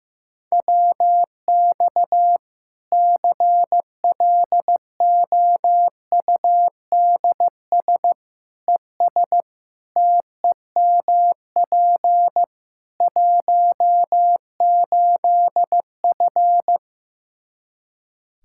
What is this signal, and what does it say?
Morse code, heard as a keyed tone, about 15 wpm, 700 Hz. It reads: WX CLOUDS ES TEMP 18F
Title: Practice QSO - Exchange 2 - Info - 15wpm